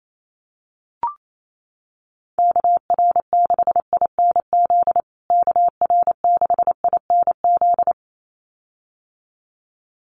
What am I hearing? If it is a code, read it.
Morse code, heard as a keyed tone, about 28 wpm, 700 Hz. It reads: KR6INZ KR6INZ
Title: USA Call Signs Repeated 2x 28wpm